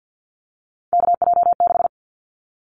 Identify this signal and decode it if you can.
Morse code, heard as a keyed tone, about 50 wpm, 700 Hz. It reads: KJ6
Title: Callsign Prefixes for Top 25 DXCC - 50wpm